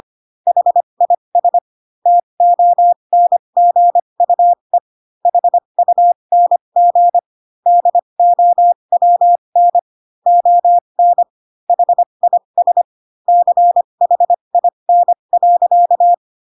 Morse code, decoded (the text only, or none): HIS TONGUE HUNG DOWN ON HIS CHIN.